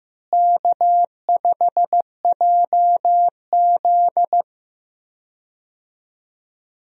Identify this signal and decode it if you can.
Morse code, heard as a keyed tone, about 15 wpm, 700 Hz. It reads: K5JZ
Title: USA Call Signs - 4 Characters - Repeated 2x - 15wpm